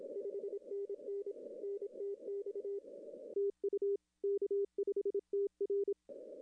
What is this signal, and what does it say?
Morse code, heard as a keyed tone, about 27 wpm, 410 Hz. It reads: 5NN NTX TU K5TR
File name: contests / arrl-160-k5tr-op-w5tn -0300